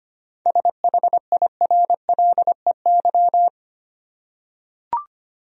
Morse code, heard as a keyed tone, about 25 wpm, 700 Hz. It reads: SHIRLEY
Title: USA Common Names - 25wpm